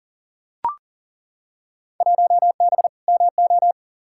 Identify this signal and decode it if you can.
Morse code, heard as a keyed tone, about 40 wpm, 700 Hz. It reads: 1B MO
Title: Field Day Exchanges - 40wpm